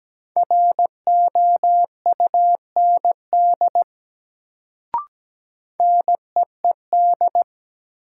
Morse code, heard as a keyed tone, about 17 wpm, 700 Hz. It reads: ROUND NEED
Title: Sentences from Top 200 Words - New Word Drill - 17wpm